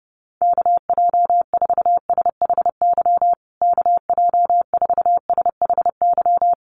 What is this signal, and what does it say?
Morse code, heard as a keyed tone, about 30 wpm, 700 Hz. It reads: KJ4SHY KJ4SHY
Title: USA Call Signs Repeated 3x 30wpm